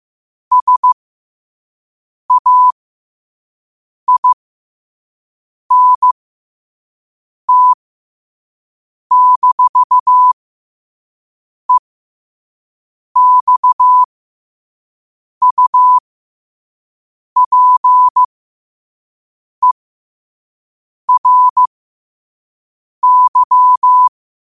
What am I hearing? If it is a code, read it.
Morse code, heard as a keyed tone, about 15 wpm, spaced as for 5 wpm, 1000 Hz. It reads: SAINT-EXUPERY
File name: qotdfiles / 5wpm